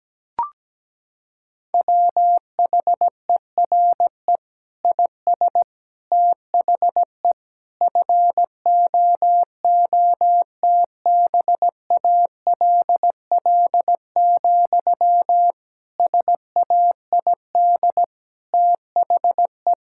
Morse code, heard as a keyed tone, about 17 wpm, 700 Hz. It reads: WHERE IS THE FOOTBALL, SAID THE
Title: Sentences from Top 1600 Words 17wpm